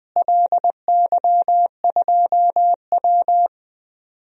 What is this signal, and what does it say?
Morse code, heard as a keyed tone, about 20 wpm, 700 Hz. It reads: LY2W
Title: DXCC-100 Call Signs - 4 Characters - Repeated 2x - 20wpm